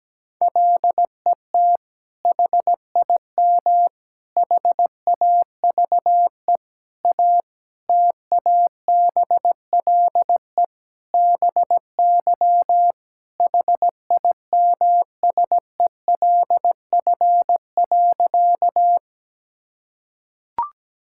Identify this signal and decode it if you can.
Morse code, heard as a keyed tone, about 17 wpm, 700 Hz. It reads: LET HIM HAVE A TABLE BY HIMSELF.
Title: Sentences from Top 1600 Words 17wpm